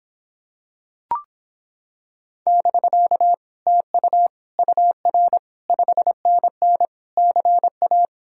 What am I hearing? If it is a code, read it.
Morse code, heard as a keyed tone, about 26 wpm, 700 Hz. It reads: <BK> TU UR 5NN CA